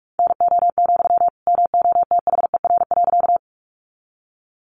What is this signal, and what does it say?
Morse code, heard as a keyed tone, about 44 wpm, 700 Hz. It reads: NO, MOTHER.
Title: Sentences from Top 200 Words 45wpm